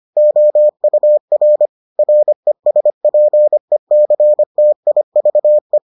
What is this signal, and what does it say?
Morse code, heard as a keyed tone, about 25 wpm, 600 Hz. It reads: OUR RESPECTIVE